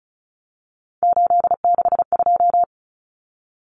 Morse code, heard as a keyed tone, about 34 wpm, 700 Hz. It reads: 862
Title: Numbers - 3-Digit 35wpm